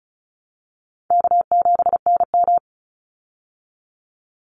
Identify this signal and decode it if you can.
Morse code, heard as a keyed tone, about 34 wpm, 700 Hz. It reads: K7NM
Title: Easy Call Signs Repeated 2x - 35wpm